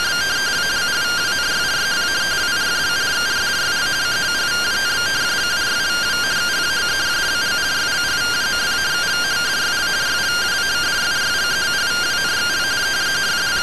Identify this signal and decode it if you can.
Two-tone teleprinter signal, about 170 Hz shift.